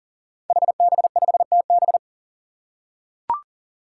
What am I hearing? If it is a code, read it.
Morse code, heard as a keyed tone, about 40 wpm, 700 Hz. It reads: HB5TB
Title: Five Challenging Letter-Number Combinations 40wpm